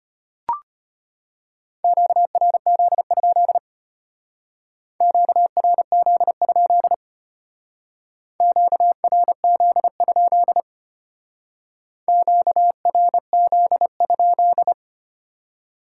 Morse code, heard as a keyed tone, about 32 wpm, 700 Hz. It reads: QRZ? QRZ? QRZ? QRZ?